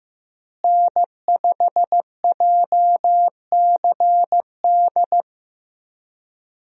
Morse code, heard as a keyed tone, about 15 wpm, 700 Hz. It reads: N5JCD
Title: USA Call Signs - 5 Characters - Repeated 2x - 15wpm